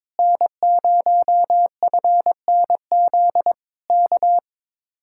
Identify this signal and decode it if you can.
Morse code, heard as a keyed tone, about 22 wpm, 700 Hz. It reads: N0FNZ K
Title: Speed-Racing Common QSO Phrases Encore - 22wpm